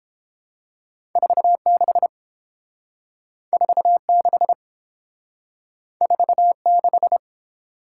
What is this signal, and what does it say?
Morse code, heard as a keyed tone, about 30 wpm, 700 Hz. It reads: 46 46 46